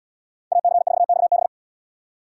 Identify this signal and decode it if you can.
Morse code, heard as a keyed tone, about 77 wpm, 700 Hz. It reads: AC4XB